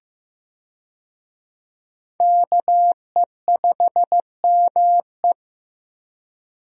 Morse code, heard as a keyed tone, about 15 wpm, 700 Hz. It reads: KE5ME
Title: USA Call Signs - 5 Characters - Repeated 3x - 15wpm